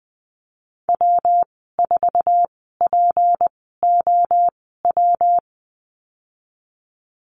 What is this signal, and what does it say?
Morse code, heard as a keyed tone, about 20 wpm, spaced as for 15 wpm, 700 Hz. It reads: W4POW
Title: 179 Lesson 29g - Slash - Call Signs 20wpm